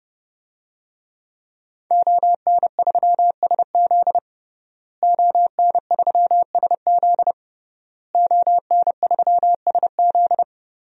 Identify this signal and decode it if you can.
Morse code, heard as a keyed tone, about 29 wpm, 700 Hz. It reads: ON3SZ ON3SZ ON3SZ
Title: DXCC-100 Call Signs - 5 Characters - Repeated 3x - 30wpm